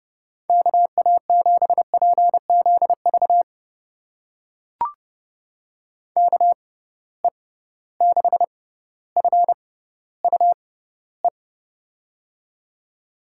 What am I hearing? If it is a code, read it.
Morse code, heard as a keyed tone, about 30 wpm, spaced as for 15 wpm, 700 Hz. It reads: KA7PZV KE6FUE
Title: USA Call Signs - 6 Characters - 6x Character Spacing - 30wpm